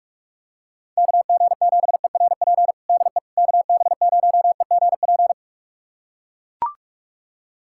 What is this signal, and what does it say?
Morse code, heard as a keyed tone, about 42 wpm, 700 Hz. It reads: KG7ERP DE KB0EGP
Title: USA Callsign DE Callsign 45wpm